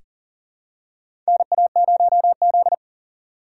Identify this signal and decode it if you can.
Morse code, heard as a keyed tone, about 38 wpm, 700 Hz. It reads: NA0Z
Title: USA Call Signs - 4 Characters - 2x Character Spacing - 40wpm